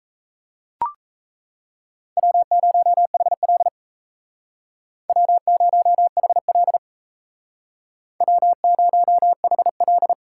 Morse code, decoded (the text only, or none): W0HL W0HL W0HL